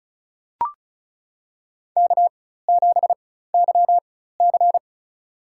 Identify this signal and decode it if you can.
Morse code, heard as a keyed tone, about 34 wpm, spaced as for 16 wpm, 700 Hz. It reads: K7YC
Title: USA Call Signs - 4 Characters - 4x Character Spacing - 35wpm